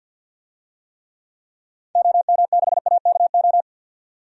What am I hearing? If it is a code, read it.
Morse code, heard as a keyed tone, about 49 wpm, 700 Hz. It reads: OM6AKO
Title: DXCC-100 Call Signs - 6 Characters - 50wpm